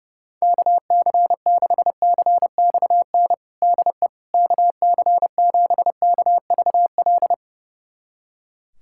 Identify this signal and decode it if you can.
Morse code, heard as a keyed tone, about 30 wpm, 700 Hz. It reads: KC6CXN DE KC7KVL
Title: USA Callsign DE Callsign - 6 Characters - 30wpm